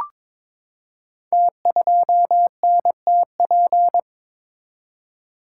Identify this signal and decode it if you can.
Morse code, heard as a keyed tone, about 23 wpm, 700 Hz. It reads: T2NTP
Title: Code Groups - 5 Letter:Numbers - 22wpm